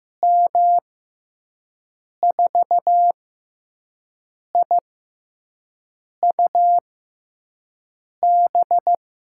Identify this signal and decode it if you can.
Morse code, heard as a keyed tone, about 16 wpm, spaced as for 5 wpm, 700 Hz. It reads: M4IUB
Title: USA Call Signs - 6 Characters - 6x Character Spacing - 15wpm